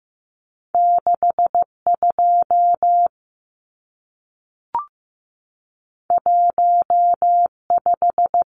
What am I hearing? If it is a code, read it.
Morse code, heard as a keyed tone, about 15 wpm, 700 Hz. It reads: 62 15
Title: Numbers - 2-Digit 15wpm